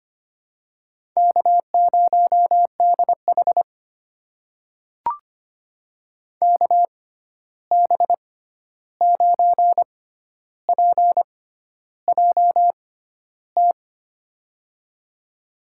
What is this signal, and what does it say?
Morse code, heard as a keyed tone, about 25 wpm, spaced as for 11 wpm, 700 Hz. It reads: K0DH KB9PJT
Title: USA Call Signs - 6x Character Spacing - 25wpm